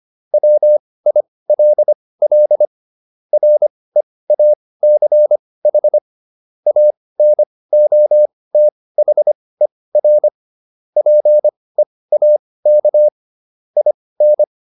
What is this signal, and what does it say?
Morse code, heard as a keyed tone, about 25 wpm, spaced as for 18 wpm, 600 Hz. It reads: WILL REACH ANOTHER PEAK IN